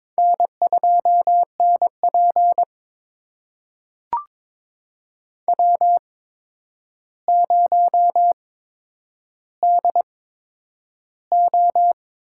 Morse code, heard as a keyed tone, about 22 wpm, spaced as for 9 wpm, 700 Hz. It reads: N2NP W0DO
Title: USA Call Signs - 4 Characters - 8x Character Spacing - 22wpm